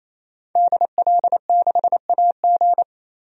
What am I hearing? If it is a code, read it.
Morse code, heard as a keyed tone, about 27 wpm, 700 Hz. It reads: DL6AG
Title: DXCC-100 Call Signs - 5 Characters - 28wpm